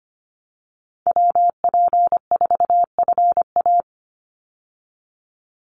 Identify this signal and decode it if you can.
Morse code, heard as a keyed tone, about 25 wpm, 700 Hz. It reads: WP4FA
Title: USA Call Signs 25wpm